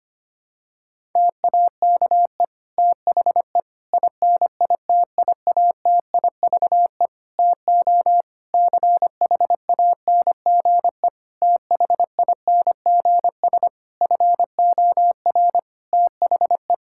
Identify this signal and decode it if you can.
Morse code, heard as a keyed tone, about 25 wpm, 700 Hz. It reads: TAKE THE INITIATIVE TO CHANGE THINGS FOR THE
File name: practice / 25wpm / Difficult Sentences with Hints - 25wpm